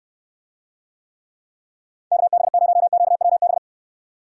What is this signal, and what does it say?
Morse code, heard as a keyed tone, about 69 wpm, 700 Hz. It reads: KB0QWB